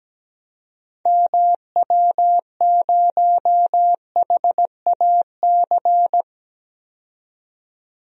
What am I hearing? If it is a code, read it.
Morse code, heard as a keyed tone, about 17 wpm, 700 Hz. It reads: MW0HAC
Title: DXCC-100 Call Signs - 6 Characters - 17wpm